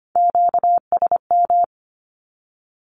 Morse code, heard as a keyed tone, about 25 wpm, 700 Hz. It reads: QSM
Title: Q-Codes with Spoken Meaning - 25wpm